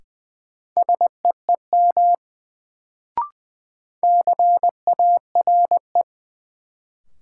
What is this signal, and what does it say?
Morse code, heard as a keyed tone, about 20 wpm, 700 Hz. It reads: SEEM CARE
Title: Sentences from Top 300 Words - New Word Drill - 20wpm